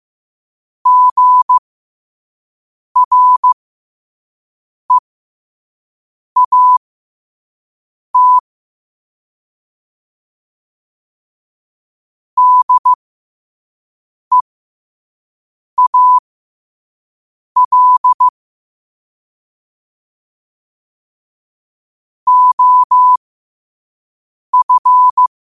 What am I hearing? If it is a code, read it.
Morse code, heard as a keyed tone, about 15 wpm, spaced as for 5 wpm, 1000 Hz. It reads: GREAT DEAL OF
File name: qotdfiles / 5wpm